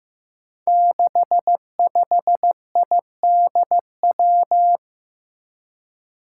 Morse code, heard as a keyed tone, about 15 wpm, 700 Hz. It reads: 65IDW
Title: Five Challenging Letter-Number Combinations 15wpm